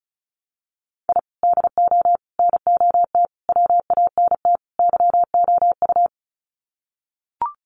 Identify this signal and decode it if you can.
Morse code, heard as a keyed tone, about 34 wpm, 700 Hz. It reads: I DO NOT WANT YOU
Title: Sets of 5 Words 35wpm